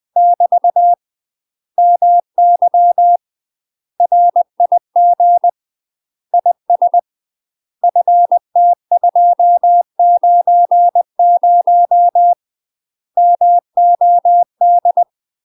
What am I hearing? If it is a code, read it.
Morse code, heard as a keyed tone, about 20 wpm, 700 Hz. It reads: <BT> MY RIG IS FT290 MOD
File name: UK MASTER QSOs TRAINING / 20WPM / UK MASTER QSOs 0059